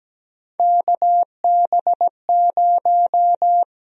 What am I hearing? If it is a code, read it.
Morse code, heard as a keyed tone, about 17 wpm, 700 Hz. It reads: KB0
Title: USA Call Sign Prefixes - Repeated 2x - 17wpm